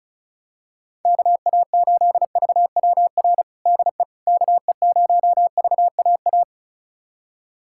Morse code, heard as a keyed tone, about 36 wpm, 700 Hz. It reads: KA8VWR DE KE0VAA
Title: USA Callsign DE Callsign - 6 Characters - 35wpm